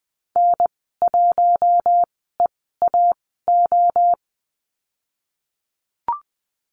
Morse code, heard as a keyed tone, about 19 wpm, spaced as for 14 wpm, 700 Hz. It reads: N1EAO
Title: 048 Lesson 9g - Number 4 - Call Signs 20wpm